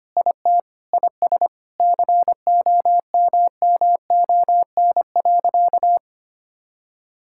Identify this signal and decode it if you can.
Morse code, heard as a keyed tone, about 25 wpm, 700 Hz. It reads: IT IS COMMON.
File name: practice / 25wpm / Sentences with 4-Word Phrases - 25wpm